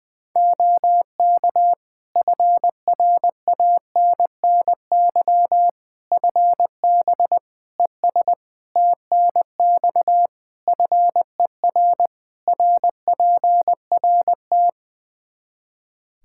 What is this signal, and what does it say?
Morse code, heard as a keyed tone, about 21 wpm, 700 Hz. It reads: OK FRANNY FB ES TNX FER RPRT
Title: Practice QSO - Exchange 2 - Info - 20wpm